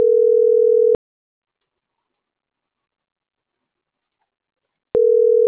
Ringback tone (North America), 440 + 480 Hz.